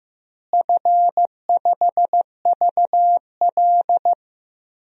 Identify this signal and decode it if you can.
Morse code, heard as a keyed tone, about 14 wpm, 700 Hz. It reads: F5VL
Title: DXCC-100 Call Signs - 4 Characters - Repeated 3x - 15wpm